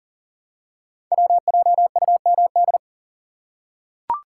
Morse code, heard as a keyed tone, about 38 wpm, 690 Hz. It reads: WJUMD
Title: Code Groups - 5 Letters - 40wpm